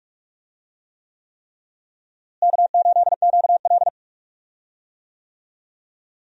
Morse code, heard as a keyed tone, about 47 wpm, 700 Hz. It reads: K8QL
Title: USA Call Signs - 4 Characters - Repeated 3x - 45wpm